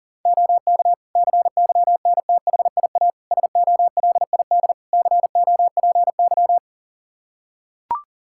Morse code, heard as a keyed tone, about 37 wpm, 700 Hz. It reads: OK CYNTHIA SOLID COPY
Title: K1USN SST QSO Practice 40wpm